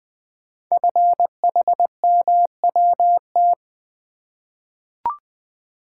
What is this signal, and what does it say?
Morse code, heard as a keyed tone, about 20 wpm, 700 Hz. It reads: FHMWT
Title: Code Groups - 5 Letters - 20wpm